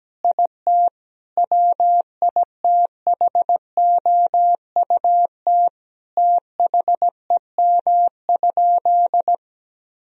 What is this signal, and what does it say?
Morse code, heard as a keyed tone, about 17 wpm, 700 Hz. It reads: IT WITHOUT THEM?
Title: Sentences from Top 1000 Words - Review Part 3 - 17wpm